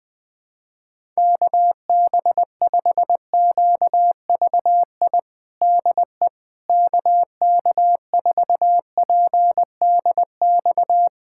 Morse code, heard as a keyed tone, about 21 wpm, 700 Hz. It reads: KB5QVI DE KK4PDX
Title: USA Callsign DE Callsign - 6 Characters - 20wpm